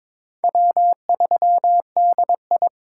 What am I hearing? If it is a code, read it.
Morse code, heard as a keyed tone, about 21 wpm, 700 Hz. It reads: W3DI
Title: USA Call Signs - 4 Characters - 4x Character Spacing - 22wpm